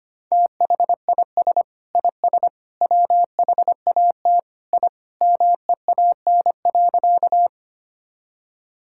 Morse code, heard as a keyed tone, about 25 wpm, 700 Hz. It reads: THIS IS WHAT I MEAN.
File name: Sentences with 3-Word Phrases - 25wpm